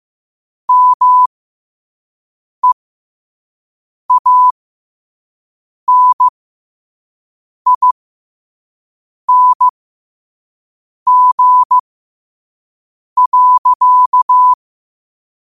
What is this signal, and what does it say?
Morse code, heard as a keyed tone, about 15 wpm, spaced as for 5 wpm, 1000 Hz. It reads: MEANING.
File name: qotdfiles / 5wpm